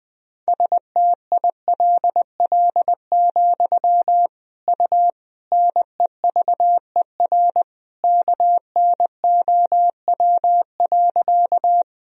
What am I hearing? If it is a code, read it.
Morse code, heard as a keyed tone, about 20 wpm, 700 Hz. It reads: STILL, U NEVER KNOW.
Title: Sentences from Top 300 Words and Abbreviations 20wpm